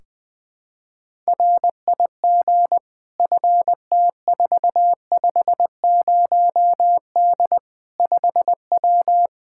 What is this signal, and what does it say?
Morse code, heard as a keyed tone, about 20 wpm, 700 Hz. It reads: RIG FT450D 5W
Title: Practice QSO - Exchange 2 - Info - 20wpm